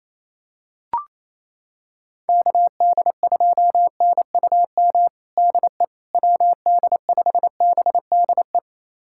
Morse code, heard as a keyed tone, about 28 wpm, 700 Hz. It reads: KD2NUM DE WD5BDE
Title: USA Callsign DE Callsign 28wpm